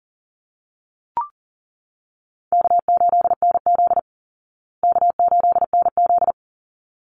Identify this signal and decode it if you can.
Morse code, heard as a keyed tone, about 40 wpm, 700 Hz. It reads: K8NZ K8NZ
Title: Easy Call Signs Repeated 2x - 40wpm